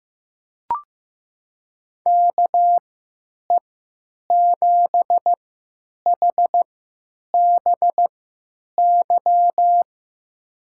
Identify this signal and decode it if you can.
Morse code, heard as a keyed tone, about 15 wpm, spaced as for 9 wpm, 700 Hz. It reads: KE7HBY
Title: USA Call Signs - 6 Characters - 3x Character Spacing - 15wpm